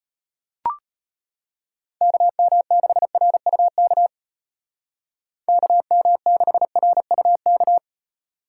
Morse code, heard as a keyed tone, about 34 wpm, 700 Hz. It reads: KM6RUK KM6RUK